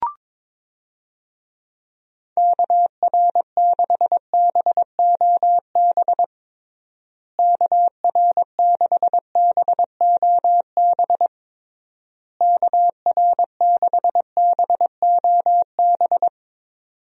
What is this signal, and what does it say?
Morse code, heard as a keyed tone, about 22 wpm, 700 Hz. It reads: KR6BOB KR6BOB KR6BOB
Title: USA Call Signs - 6 Characters - Repeated 3x - 22wpm